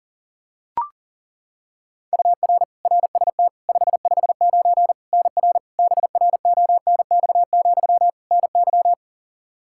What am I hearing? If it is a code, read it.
Morse code, heard as a keyed tone, about 40 wpm, 700 Hz. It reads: UR RST 559 NR BRONX, NY
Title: Practice QSO - Exchange 1 - Beginning - 40wpm